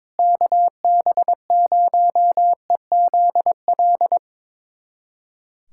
Morse code, heard as a keyed tone, about 22 wpm, 700 Hz. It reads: KB0EZL
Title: USA Callsign DE Callsign - 6 Characters - 22wpm